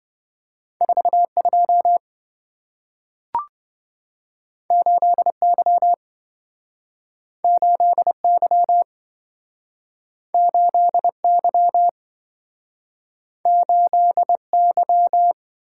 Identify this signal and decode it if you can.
Morse code, heard as a keyed tone, about 27 wpm, 700 Hz. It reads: 42 8Y 8Y 8Y 8Y